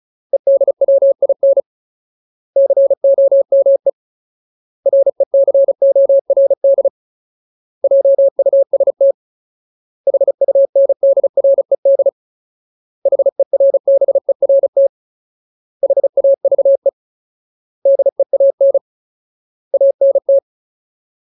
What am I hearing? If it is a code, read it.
Morse code, heard as a keyed tone, about 36 wpm, 550 Hz. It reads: EDWIN COME RECORD JUST HUNDRED HERBERT HAVE DEAN ANT
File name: Ultimate Word List / 35WPM Ultimate